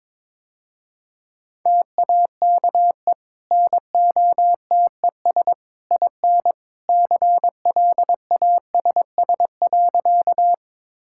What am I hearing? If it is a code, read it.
Morse code, heard as a keyed tone, about 21 wpm, 700 Hz. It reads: TAKE NOTES IN CLASS.
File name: practice / 22wpm / Easy Sentences with Hints - 22wpm